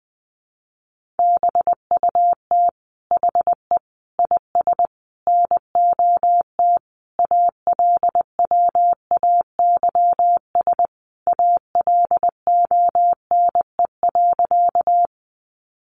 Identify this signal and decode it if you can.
Morse code, heard as a keyed tone, about 20 wpm, 700 Hz. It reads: BUT HE IS NOT ALWAYS ALONE.
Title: Sentences from Top 1000 Words 20wpm